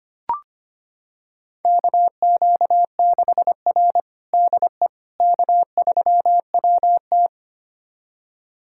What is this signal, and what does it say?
Morse code, heard as a keyed tone, about 25 wpm, 700 Hz. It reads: KQ6R DE K3WT
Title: USA Callsign DE Callsign - 4 Characters - 25wpm